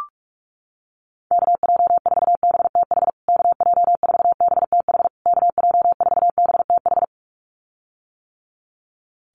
Morse code, heard as a keyed tone, about 44 wpm, 690 Hz. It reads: KJ4BTH KJ4BTH KJ4BTH
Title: USA Call Signs Repeated 3x 45wpm